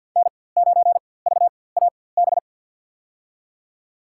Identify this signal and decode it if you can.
Morse code, heard as a keyed tone, about 49 wpm, spaced as for 23 wpm, 700 Hz. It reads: N9VAB
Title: USA Call Signs - 5 Characters - 4x Character Spacing - 50wpm